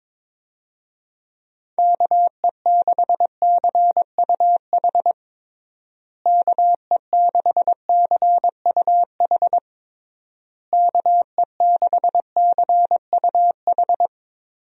Morse code, heard as a keyed tone, about 22 wpm, 700 Hz. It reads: KE6CUH KE6CUH KE6CUH
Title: USA Call Signs - 6 Characters - Repeated 3x - 22wpm